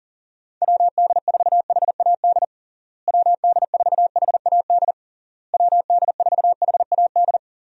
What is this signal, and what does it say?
Morse code, heard as a keyed tone, about 40 wpm, 700 Hz. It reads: WD4HAD WD4HAD WD4HAD
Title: USA Call Signs - 6 Characters - Repeated 3x - 40wpm